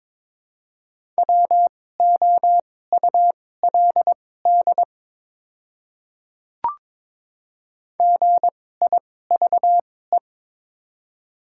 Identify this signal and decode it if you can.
Morse code, heard as a keyed tone, about 22 wpm, spaced as for 16 wpm, 700 Hz. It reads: WOULD GIVE
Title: 2x Character Spacing - Top 200 Words - 22wpm